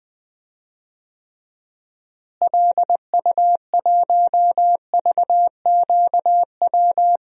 Morse code, heard as a keyed tone, about 20 wpm, 700 Hz. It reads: LU1VQW
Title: DXCC-100 Call Signs - 6 Characters - Repeated 2x - 20wpm